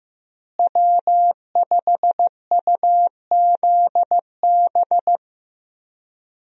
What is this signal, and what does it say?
Morse code, heard as a keyed tone, about 15 wpm, 700 Hz. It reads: W5UZB
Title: USA Call Signs - 5 Characters - Repeated 3x - 15wpm